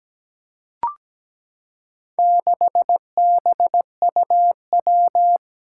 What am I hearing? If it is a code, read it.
Morse code, heard as a keyed tone, about 17 wpm, 700 Hz. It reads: 6BUW
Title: Four Challenging Letter-Number Combinations 17wpm